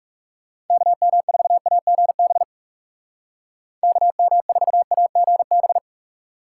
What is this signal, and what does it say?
Morse code, heard as a keyed tone, about 36 wpm, 700 Hz. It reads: KM4AGB KM4AGB
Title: Speed Racing - USA 2x3 Call Signs 30wpm